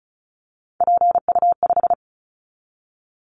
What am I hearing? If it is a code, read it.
Morse code, heard as a keyed tone, about 34 wpm, 700 Hz. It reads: PU5
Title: Callsign Prefixes for Top 100 DXCC - 35wpm